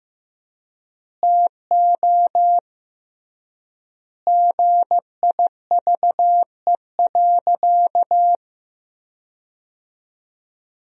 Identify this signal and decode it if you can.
Morse code, heard as a keyed tone, about 15 wpm, 700 Hz. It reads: TO GIVE.
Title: 3x Word-Spacing - Sentences from Top 200 Words - 15wpm